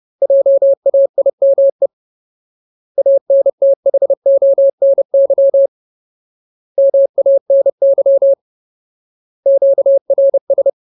Morse code, heard as a keyed tone, about 30 wpm, 550 Hz. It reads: JAIME ANTHONY MANY QRS
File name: Ultimate Word List / 30WPM Ultimate